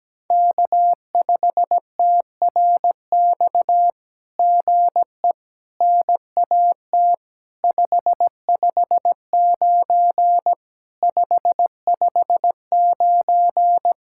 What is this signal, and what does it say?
Morse code, heard as a keyed tone, about 17 wpm, 700 Hz. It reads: K5TRX GE NAT 559 559
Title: SOTA QSO Practice 17wpm